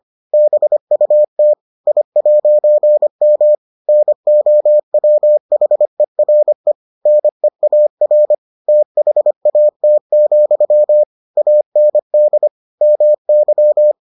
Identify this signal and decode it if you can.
Morse code, heard as a keyed tone, about 25 wpm, 600 Hz. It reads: BUT I'M NOWHERE NEAR THAT, AND MY